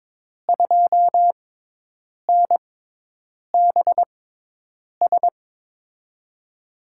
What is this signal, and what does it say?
Morse code, heard as a keyed tone, about 23 wpm, spaced as for 8 wpm, 700 Hz. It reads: 2NBS
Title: USA Call Signs - 6x Character Spacing - 22wpm